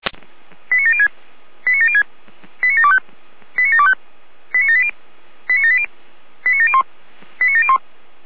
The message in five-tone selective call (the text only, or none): 89706 89706 89725 89725 89780 89780 89701 89701